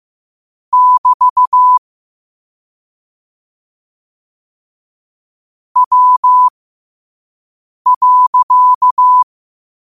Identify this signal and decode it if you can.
Morse code, heard as a keyed tone, about 15 wpm, spaced as for 5 wpm, 1000 Hz. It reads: <BT> W.
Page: 5 WPM morse code quotes